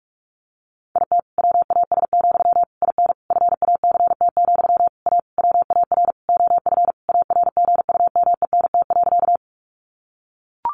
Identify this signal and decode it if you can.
Morse code, heard as a keyed tone, about 44 wpm, 700 Hz. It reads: IT WAS, IN FACT, A WAR OF ARGUMENT.
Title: Sentences from Top 1400 Words 45wpm